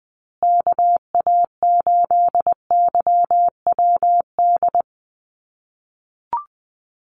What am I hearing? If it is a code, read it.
Morse code, heard as a keyed tone, about 20 wpm, 700 Hz. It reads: KA8YWD
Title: USA Call Signs Repeated 2x 20wpm